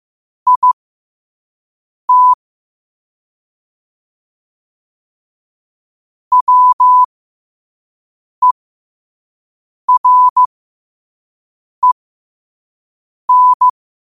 Morse code, heard as a keyed tone, about 15 wpm, spaced as for 5 wpm, 1000 Hz. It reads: IT WEREN
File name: qotdfiles / 5wpm